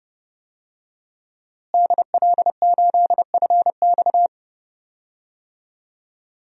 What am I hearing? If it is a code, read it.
Morse code, heard as a keyed tone, about 30 wpm, 700 Hz. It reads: DL8FX
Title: DXCC-100 Call Signs - 5 Characters - Repeated 3x - 30wpm